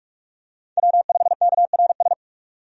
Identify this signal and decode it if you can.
Morse code, heard as a keyed tone, about 43 wpm, 700 Hz. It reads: W5KRS
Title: USA Call Signs - 5 Characters - 45wpm